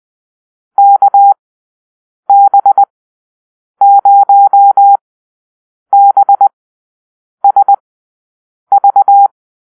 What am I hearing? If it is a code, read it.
Morse code, heard as a keyed tone, about 20 wpm, spaced as for 7 wpm, 800 Hz. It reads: KB0BSV
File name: arrl / Call-Sign Drill-10